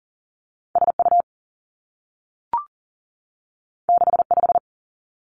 Morse code, heard as a keyed tone, about 39 wpm, 700 Hz. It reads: SU 65
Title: Two Challenging Letter-Number Combinations 40wpm